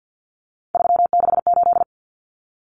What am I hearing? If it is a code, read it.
Morse code, heard as a keyed tone, about 50 wpm, 700 Hz. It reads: 368
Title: Numbers - 3-Digit 50wpm